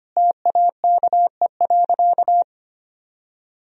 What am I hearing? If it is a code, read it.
Morse code, heard as a keyed tone, about 25 wpm, 700 Hz. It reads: TAKE.
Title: Sentences from Top 1000 Words - Review Part 4 - 25wpm